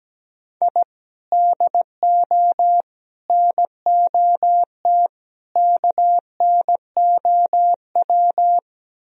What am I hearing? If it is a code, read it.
Morse code, heard as a keyed tone, about 17 wpm, 700 Hz. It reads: I DO NOT KNOW
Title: Speed-Racing - Sets of 4 Words 17wpm